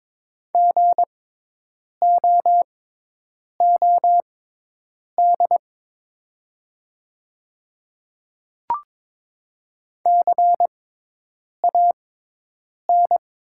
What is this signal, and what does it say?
Morse code, heard as a keyed tone, about 22 wpm, spaced as for 8 wpm, 700 Hz. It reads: GOOD CAN
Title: 6x Character Spacing - Top 300 Words - 22wpm